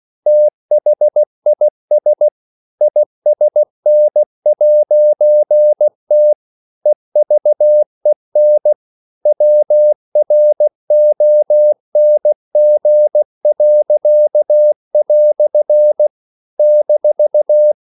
Morse code, heard as a keyed tone, about 16 wpm, 600 Hz. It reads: THIS ISN'T EVEN WRONG." -
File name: CW / Practice / EnglishQuotations0416